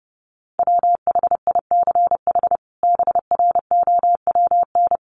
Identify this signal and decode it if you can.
Morse code, heard as a keyed tone, about 30 wpm, 700 Hz. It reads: WHICH BROWN
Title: Sentences from Top 1200 Words 30wpm